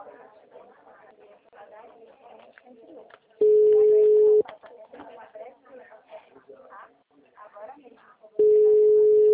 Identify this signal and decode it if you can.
Ringback tone (Europe, 425 Hz), with speech.
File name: out- -2003- - - .48738